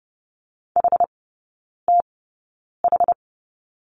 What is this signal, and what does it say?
Morse code, heard as a keyed tone, about 30 wpm, spaced as for 9 wpm, 700 Hz. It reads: HTH
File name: Lesson 17e - Letter u - Three Characters 30wpm